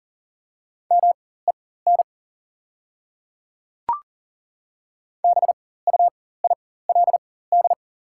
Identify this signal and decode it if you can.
Morse code, heard as a keyed tone, about 38 wpm, spaced as for 19 wpm, 690 Hz. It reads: MEN BUILD
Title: 4x Character Spacing - Top 200 Words - 40wpm